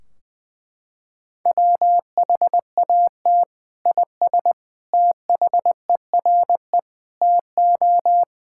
Morse code, heard as a keyed tone, about 21 wpm, 700 Hz. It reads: WHAT IS THERE TO
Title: Sentences from Top 1600 Words 20wpm